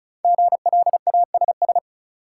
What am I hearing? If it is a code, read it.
Morse code, heard as a keyed tone, about 36 wpm, 700 Hz. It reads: GLASS
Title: Sentences from Top 1000 Words - Review Part 3 - New Word Drill - 35wpm